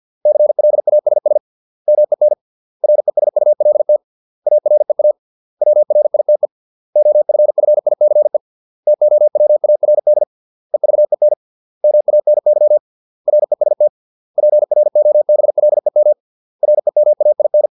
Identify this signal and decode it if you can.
Morse code, heard as a keyed tone, about 50 wpm, 600 Hz. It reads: CLASS MEN RESULT AREA WRITE OFFICE TOWARD EVEN MANY REST PROBLEM REMAIN